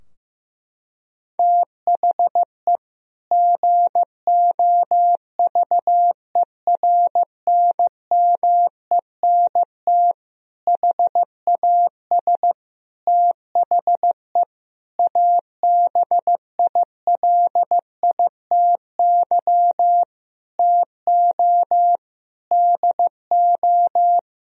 Morse code, heard as a keyed tone, about 16 wpm, 700 Hz. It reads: THE GOVERNMENT HAS THE ABILITY TO DO
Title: Sentences from Top 1000 Words - Review Part 3 - 15wpm